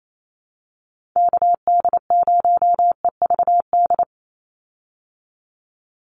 Morse code, heard as a keyed tone, about 28 wpm, 700 Hz. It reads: KD0EVD
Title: USA Call Signs 28wpm